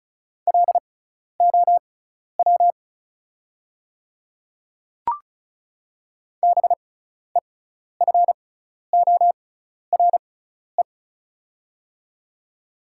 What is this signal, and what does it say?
Morse code, heard as a keyed tone, about 34 wpm, spaced as for 12 wpm, 700 Hz. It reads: LOW BEFORE
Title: 6x Character Spacing - Top 200 Words - 35wpm